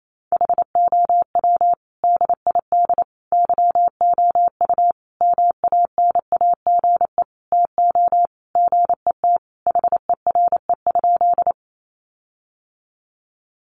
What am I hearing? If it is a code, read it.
Morse code, heard as a keyed tone, about 28 wpm, 700 Hz. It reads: HOW DID YOU MANAGE TO GET HERE?
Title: Sentences from Top 1400 Words 28wpm